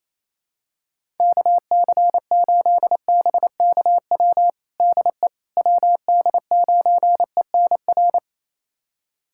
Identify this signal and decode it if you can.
Morse code, heard as a keyed tone, about 28 wpm, 700 Hz. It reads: KC8BKW DE WD9ENR
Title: USA Callsign DE Callsign - 6 Characters - 28wpm